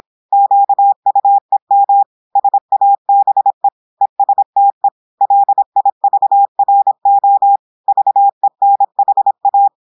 Morse code, heard as a keyed tone, about 26 wpm, 800 Hz. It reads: QUEM SABE ESTE LIVRO VENHA
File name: audios / portugais / portugais texte en clair 1 vitesse 26 mots minutes